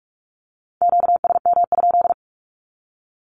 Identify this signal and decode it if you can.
Morse code, heard as a keyed tone, about 46 wpm, 700 Hz. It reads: QSM?
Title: Q-Codes with Spoken Meaning - 45wpm